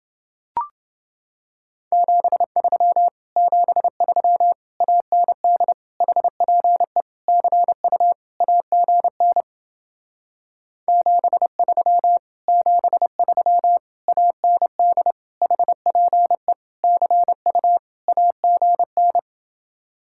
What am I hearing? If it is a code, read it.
Morse code, heard as a keyed tone, about 28 wpm, 700 Hz. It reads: 73 73 AND HPE CU AGN 73 73 AND HPE CU AGN